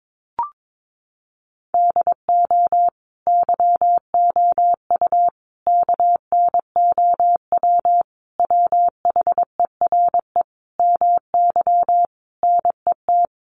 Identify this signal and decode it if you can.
Morse code, heard as a keyed tone, about 22 wpm, 700 Hz. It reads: DO YOU KNOW WHERE MY NET
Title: Sentences from Top 1400 Words 22wpm